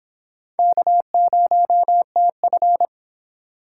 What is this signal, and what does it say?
Morse code, heard as a keyed tone, about 25 wpm, 700 Hz. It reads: K0TF